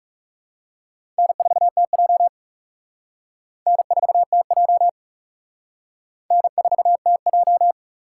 Morse code, heard as a keyed tone, about 34 wpm, 700 Hz. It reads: N4TJ N4TJ N4TJ
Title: Speed Racing - USA Call Signs 30wpm